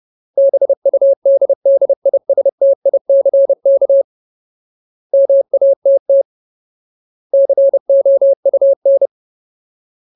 Morse code, heard as a keyed tone, about 30 wpm, 550 Hz. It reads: BUDDISTICK MATT COUN
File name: Ultimate Word List / 30WPM Ultimate